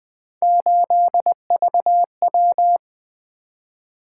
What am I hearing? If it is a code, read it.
Morse code, heard as a keyed tone, about 20 wpm, 700 Hz. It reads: 8VW
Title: Three Letter-Number Combinations - 20wpm